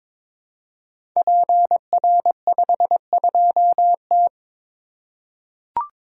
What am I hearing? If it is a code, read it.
Morse code, heard as a keyed tone, about 23 wpm, 700 Hz. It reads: PR52T
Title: Code Groups - 5 Letter:Numbers - 22wpm